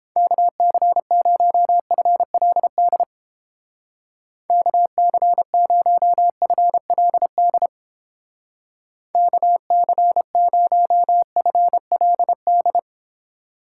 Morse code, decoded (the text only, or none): KC0FLD KC0FLD KC0FLD